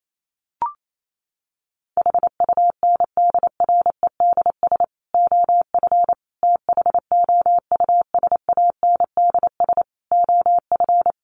Morse code, heard as a keyed tone, about 28 wpm, 700 Hz. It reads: HUNDREDS OF THOUSANDS OF
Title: Sets of 5 Words Encore 28wpm